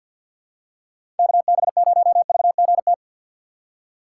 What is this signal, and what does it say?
Morse code, heard as a keyed tone, about 46 wpm, 690 Hz. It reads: KB0VGT
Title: USA Call Signs - 6 Characters - 3x Character Spacing - 50wpm